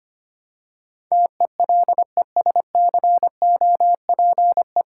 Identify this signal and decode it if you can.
Morse code, heard as a keyed tone, about 26 wpm, 700 Hz. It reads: TELESCOPE
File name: practice / 25wpm / Difficult Words - Speed-Racing with Hints - 25wpm